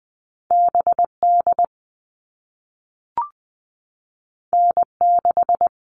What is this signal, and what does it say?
Morse code, heard as a keyed tone, about 20 wpm, 700 Hz. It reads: BD N6
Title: All Cut Numbers - 2-Digit 20wpm